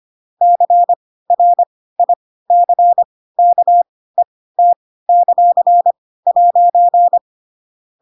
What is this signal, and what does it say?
Morse code, heard as a keyed tone, about 25 wpm, spaced as for 16 wpm, 700 Hz. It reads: CRICKET;'
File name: Pinocchio / Pinocchio ch7 10 25 ik1hge